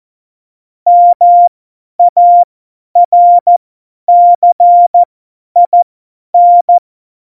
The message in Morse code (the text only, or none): MARCIN